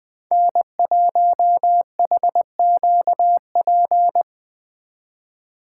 Morse code, heard as a keyed tone, about 20 wpm, 700 Hz. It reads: N1HQP
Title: Speed Racing - USA 1x3 Call Signs 17wpm